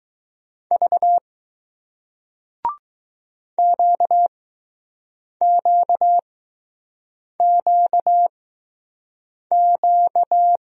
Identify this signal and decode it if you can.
Morse code, heard as a keyed tone, about 20 wpm, spaced as for 6 wpm, 700 Hz. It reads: V QQQQ